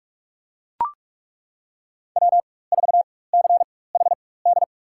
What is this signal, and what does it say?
Morse code, heard as a keyed tone, about 42 wpm, spaced as for 21 wpm, 700 Hz. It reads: W4CHD
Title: USA Call Signs - 5 Characters - 4x Character Spacing - 45wpm